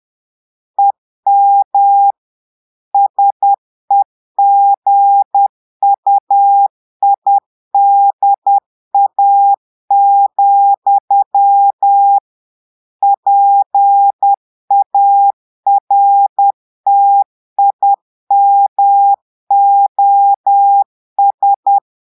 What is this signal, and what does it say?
Morse code, heard as a keyed tone, about 10 wpm, 800 Hz. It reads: EM SEGUIDA, PARTIMOS
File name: portugais texte en clair 2 vitesse 10 mots minutes